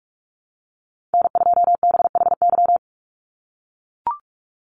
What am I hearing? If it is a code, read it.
Morse code, heard as a keyed tone, about 44 wpm, 700 Hz. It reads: N2BHY
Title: USA Call Signs 45wpm